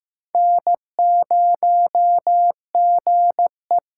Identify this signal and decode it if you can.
Morse code, heard as a keyed tone, about 15 wpm, 700 Hz. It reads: N0GE
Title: USA Call Signs - 4 Characters - 8x Character Spacing - 15wpm